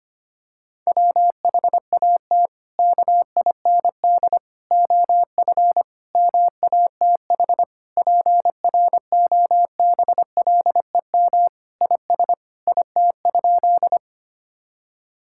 Morse code, heard as a keyed tone, about 25 wpm, 700 Hz. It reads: WHAT KIND OF MATH PROBLEM IS IT?
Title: Sentences from Top 1000 Words - Review Part 6 - 25wpm